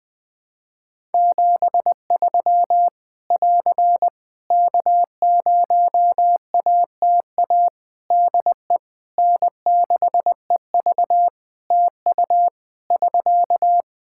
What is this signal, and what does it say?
Morse code, heard as a keyed tone, about 20 wpm, 700 Hz. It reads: 73 <AR> K0ATA DE N6EV TU <SK>
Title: Practice QSO - Exchange 4 - Ending - 20wpm